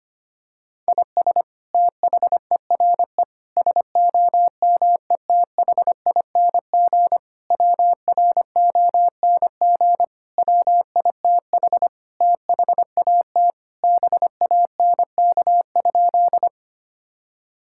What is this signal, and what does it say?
Morse code, heard as a keyed tone, about 25 wpm, 700 Hz. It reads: IS THERE SOMETHING WRONG WITH THAT BANK?
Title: Sentences from Top 1000 Words - Review Part 6 - 25wpm